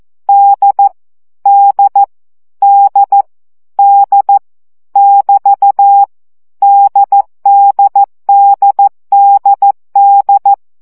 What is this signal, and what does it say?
Morse code, heard as a keyed tone, about 14 wpm, 800 Hz. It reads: D D D D <BT> DDDDD
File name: CW gyakorlas d 12 baud